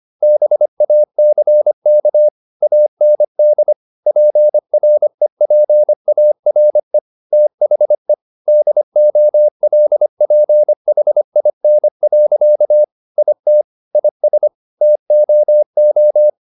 Morse code, heard as a keyed tone, about 25 wpm, 600 Hz. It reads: BACK AND PREPARE THE DOLPHIN. IT IS TOO